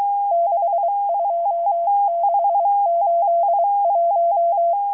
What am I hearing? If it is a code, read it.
Morse code, heard as a keyed tone, about 23 wpm, 680 Hz. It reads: 62681